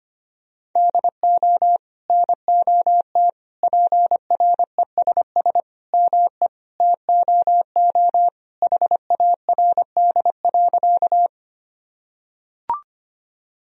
Morse code, heard as a keyed tone, about 25 wpm, 700 Hz. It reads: DO NOT PRESS ME TOO HARD.
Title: Sentences from Top 1000 Words - Review Part 1 - 25wpm